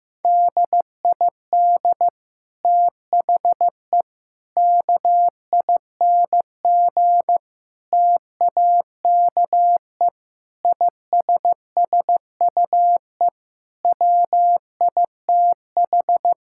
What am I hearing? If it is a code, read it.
Morse code, heard as a keyed tone, about 16 wpm, 700 Hz. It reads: DID THE KING TAKE ISSUE WITH
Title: Sentences from Top 1000 Words - Review Part 6 - 15wpm